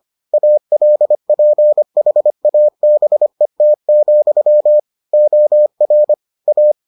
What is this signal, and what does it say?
Morse code, heard as a keyed tone, about 25 wpm, 600 Hz. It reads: ALPHABET, OR A